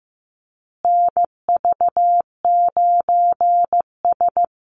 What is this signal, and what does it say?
Morse code, heard as a keyed tone, about 15 wpm, 700 Hz. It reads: NV9S
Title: USA Call Signs Repeated 3x 15wpm